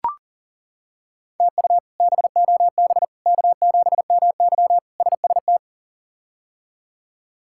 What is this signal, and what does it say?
Morse code, heard as a keyed tone, about 40 wpm, 700 Hz. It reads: TU BOB K7MY SST
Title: K1USN SST QSO Practice - 40wpm